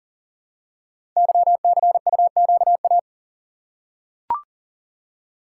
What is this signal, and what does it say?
Morse code, heard as a keyed tone, about 38 wpm, 700 Hz. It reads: YCUQA
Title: Code Groups - 5 Letters - 40wpm